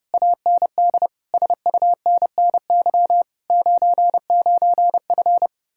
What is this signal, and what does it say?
Morse code, heard as a keyed tone, about 33 wpm, 700 Hz. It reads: AND SUNNY 99F